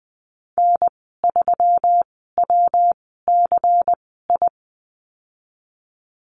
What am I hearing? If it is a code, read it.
Morse code, heard as a keyed tone, about 20 wpm, spaced as for 15 wpm, 700 Hz. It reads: N3WCI
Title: 179 Lesson 29g - Slash - Call Signs 20wpm